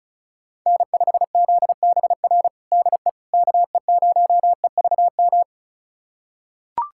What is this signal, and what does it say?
Morse code, heard as a keyed tone, about 34 wpm, 700 Hz. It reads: N5ZBR DE KE0EVM
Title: USA Callsign DE Callsign 35wpm